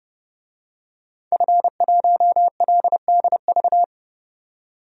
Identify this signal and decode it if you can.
Morse code, heard as a keyed tone, about 30 wpm, 700 Hz. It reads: F1LDV
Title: DXCC-100 Call Signs - 5 Characters - 30wpm